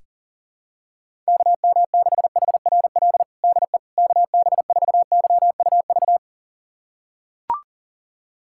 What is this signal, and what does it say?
Morse code, heard as a keyed tone, about 40 wpm, 700 Hz. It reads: KM6HRL DE KB4YUV
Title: USA Callsign DE Callsign 40wpm